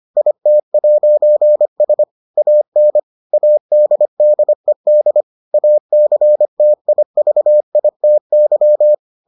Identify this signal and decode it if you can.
Morse code, heard as a keyed tone, about 25 wpm, 600 Hz. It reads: IT'S AN ADDED ACTIVITY